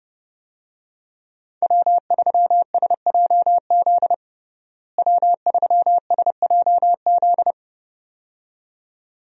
Morse code, heard as a keyed tone, about 31 wpm, 700 Hz. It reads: W3SJZ W3SJZ
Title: USA Call Signs - 5 Characters - Repeated 2x - 30wpm